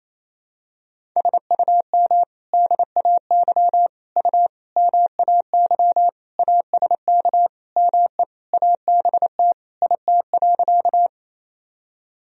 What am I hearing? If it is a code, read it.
Morse code, heard as a keyed tone, about 28 wpm, 700 Hz. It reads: SUM DAY U MAY ASK ME ABT IT.
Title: Sentences from Top 200 Words and Abbreviations 28wpm